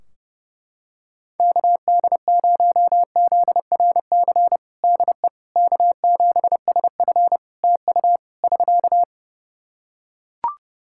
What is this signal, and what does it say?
Morse code, heard as a keyed tone, about 31 wpm, 700 Hz. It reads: KD0ZRC DE K7SF TU <SK>
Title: Practice QSO - Exchange 4 - Ending - 30wpm